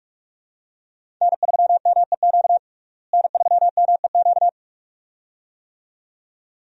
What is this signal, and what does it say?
Morse code, heard as a keyed tone, about 42 wpm, 700 Hz. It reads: N3MEQ N3MEQ
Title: USA Call Signs - 5 Characters - Repeated 2x - 45wpm